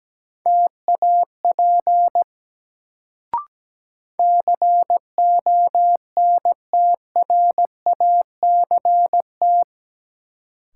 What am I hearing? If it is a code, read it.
Morse code, heard as a keyed tone, about 17 wpm, 700 Hz. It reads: TAP CONTRACT
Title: Sentences from Top 1400 Words - New Word Drill - 17wpm